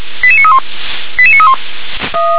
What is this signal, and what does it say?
Five-tone selective call (ZVEI), heard as quick runs of five tones, tones of about 70 ms. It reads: 88931 88931
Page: SDR/ZVEI